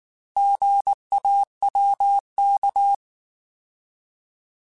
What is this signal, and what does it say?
Morse code, heard as a keyed tone, about 19 wpm, 800 Hz. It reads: GAWK